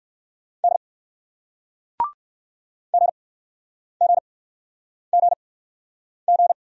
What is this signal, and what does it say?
Morse code, heard as a keyed tone, about 57 wpm, spaced as for 9 wpm, 700 Hz. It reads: D GGGG